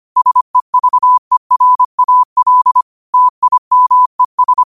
Morse code, heard as a keyed tone, about 25 wpm, 1000 Hz. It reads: SEVERAL TIMES